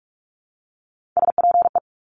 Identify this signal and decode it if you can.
Morse code, heard as a keyed tone, about 46 wpm, 700 Hz. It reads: SPE
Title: Top Three Letter Combinations in Words 45wpm